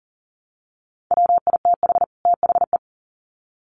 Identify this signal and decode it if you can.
Morse code, heard as a keyed tone, about 40 wpm, 700 Hz. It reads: WITH THE
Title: Sets of 2 Words 40wpm